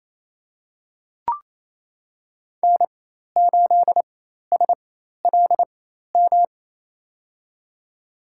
Morse code, heard as a keyed tone, about 28 wpm, spaced as for 13 wpm, 700 Hz. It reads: N8SLM
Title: USA Call Signs - 5 Characters - 4x Character Spacing - 28wpm